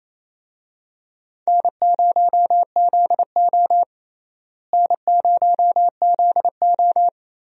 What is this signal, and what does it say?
Morse code, heard as a keyed tone, about 28 wpm, 700 Hz. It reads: N0ZO N0ZO
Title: USA Call Signs - 4 Characters - Repeated 3x - 28wpm